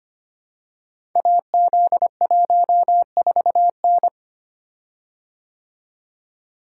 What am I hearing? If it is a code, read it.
Morse code, heard as a keyed tone, about 25 wpm, 700 Hz. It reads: AZ14N
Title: Code Groups - 5 Letter:Numbers - 25wpm